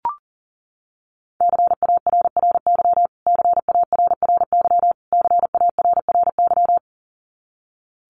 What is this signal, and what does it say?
Morse code, heard as a keyed tone, about 40 wpm, 700 Hz. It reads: CARRY CARRY CARRY
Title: Top 500 Words Repeated 40wpm